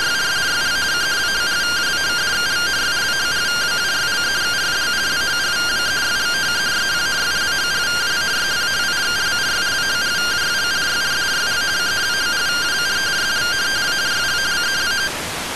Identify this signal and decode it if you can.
Two-tone teleprinter signal, about 170 Hz shift.